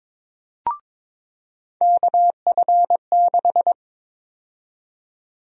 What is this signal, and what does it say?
Morse code, heard as a keyed tone, about 22 wpm, 700 Hz. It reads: KF6
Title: USA Call Sign Prefixes - 22wpm